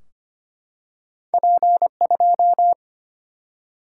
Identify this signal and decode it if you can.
Morse code, heard as a keyed tone, about 25 wpm, 700 Hz. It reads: P2
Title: Single-Letter Single-Number 25wpm